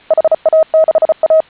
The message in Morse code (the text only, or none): HA6A